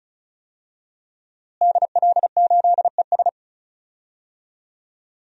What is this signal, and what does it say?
Morse code, heard as a keyed tone, about 36 wpm, 700 Hz. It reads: DL8ES
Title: DXCC-100 Call Signs - 5 Characters - Repeated 3x - 35wpm